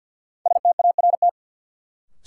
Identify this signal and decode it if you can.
Morse code, heard as a keyed tone, about 52 wpm, 700 Hz. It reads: START
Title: Sentences from Top 300 Words - New Word Drill - 50wpm